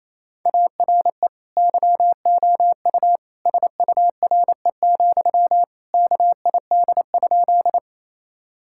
Morse code, heard as a keyed tone, about 28 wpm, 700 Hz. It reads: ARE YOU SURE, KID?
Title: Sentences from Top 1600 Words 28wpm